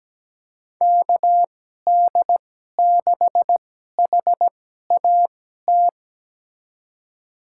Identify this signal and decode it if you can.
Morse code, heard as a keyed tone, about 17 wpm, spaced as for 12 wpm, 700 Hz. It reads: KD6HAT
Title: USA Call Signs - 6 Characters - 2x Character Spacing - 17wpm